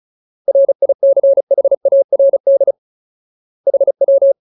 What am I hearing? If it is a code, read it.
Morse code, heard as a keyed tone, about 34 wpm, 550 Hz. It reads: RICHARD HW
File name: Ultimate Word List / 35WPM Ultimate